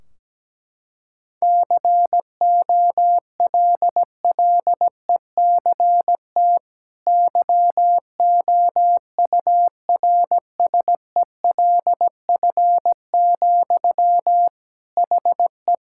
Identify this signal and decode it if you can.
Morse code, heard as a keyed tone, about 17 wpm, 700 Hz. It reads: COLLECT YOURSELF, HE
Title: Sentences from Top 1600 Words 17wpm